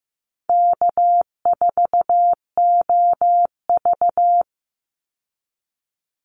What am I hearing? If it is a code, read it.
Morse code, heard as a keyed tone, about 15 wpm, 700 Hz. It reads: K4OV
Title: Easy Call Signs Repeated 3x - 15wpm